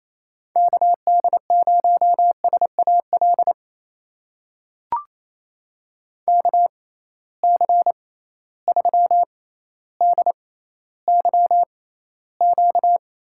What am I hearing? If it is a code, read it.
Morse code, heard as a keyed tone, about 28 wpm, spaced as for 14 wpm, 700 Hz. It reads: KD0SAL KC3DYQ
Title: USA Call Signs - 6 Characters - 6x Character Spacing - 28wpm